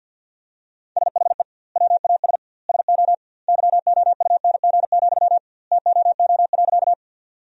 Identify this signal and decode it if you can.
Morse code, heard as a keyed tone, about 50 wpm, 700 Hz. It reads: SHE WAS SO YOUNG, TOO.
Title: 2x Word-Spacing - Sentences from Top 400 Words - 50wpm